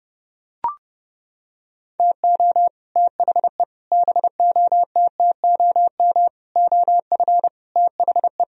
Morse code, heard as a keyed tone, about 30 wpm, 700 Hz. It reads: TO THE BOTTOM OF THE